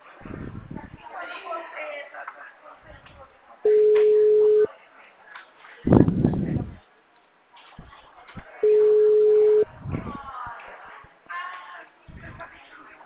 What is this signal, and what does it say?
Ringback tone (Europe, 425 Hz), with speech.
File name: out- -2016- - - .65342